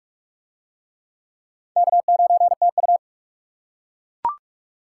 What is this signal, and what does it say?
Morse code, heard as a keyed tone, about 37 wpm, 700 Hz. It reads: K9TU
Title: Speed Racing - USA 1x2 Call Signs 30wpm